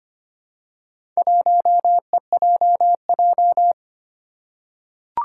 Morse code, heard as a keyed tone, about 25 wpm, 700 Hz. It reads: 1EJJ
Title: Four Challenging Letter-Number Combinations 25wpm